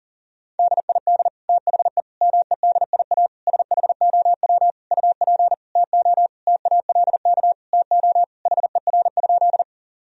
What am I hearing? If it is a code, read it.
Morse code, heard as a keyed tone, about 40 wpm, 700 Hz. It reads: DID THE MEDIA SHOW UP TO TALK TO HER?
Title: Sentences from Top 1000 Words - Review Part 4 - 40wpm